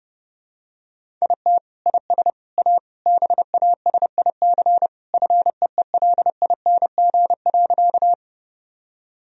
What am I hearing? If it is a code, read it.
Morse code, heard as a keyed tone, about 30 wpm, 700 Hz. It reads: IT IS A BASIC FEELING.
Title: Sentences from Top 1000 Words - Review Part 3 - 30wpm